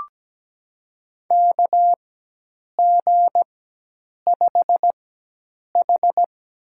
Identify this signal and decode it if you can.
Morse code, heard as a keyed tone, about 17 wpm, spaced as for 8 wpm, 700 Hz. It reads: KG5H
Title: USA Call Signs - 4x Character Spacing - 17wpm